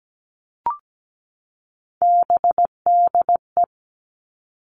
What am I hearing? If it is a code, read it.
Morse code, heard as a keyed tone, about 17 wpm, 700 Hz. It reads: BDE
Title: All Cut Numbers - 3-Digit 17wpm